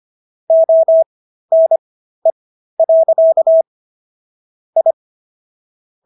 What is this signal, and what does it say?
Morse code, heard as a keyed tone, about 25 wpm, spaced as for 13 wpm, 650 Hz. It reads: ONE. I
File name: SS106 13